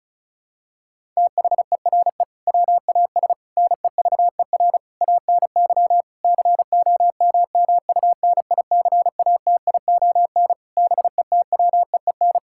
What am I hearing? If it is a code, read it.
Morse code, heard as a keyed tone, about 36 wpm, 700 Hz. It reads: THERE WAS NEVER ANY COMMUNICATION BETWEEN
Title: Sentences from Top 1000 Words - Review Part 6 - 35wpm